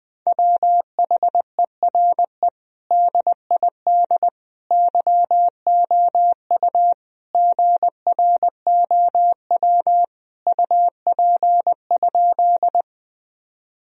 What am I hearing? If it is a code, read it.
Morse code, heard as a keyed tone, about 20 wpm, 700 Hz. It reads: WHERE DID YOU GROW UP?
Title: Sentences from Top 1000 Words - Review Part 1 - 20wpm